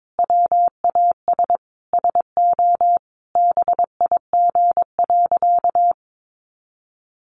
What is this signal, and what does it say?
Morse code, heard as a keyed tone, about 22 wpm, 700 Hz. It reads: WAS SO BIG.
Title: Sentences from Top 600 Words 22wpm